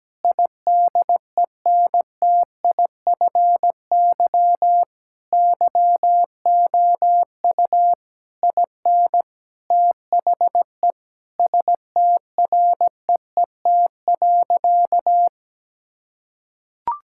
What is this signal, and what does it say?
Morse code, heard as a keyed tone, about 17 wpm, 700 Hz. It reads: IDENTIFY YOU IN THE STREET.
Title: Sentences from Top 1000 Words - Review Part 6 - 17wpm